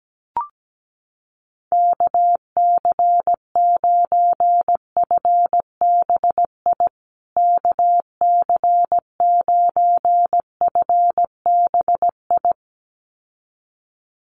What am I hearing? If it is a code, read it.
Morse code, heard as a keyed tone, about 17 wpm, 700 Hz. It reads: KC9FBI KC9FBI
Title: USA Call Signs Repeated 2x 17wpm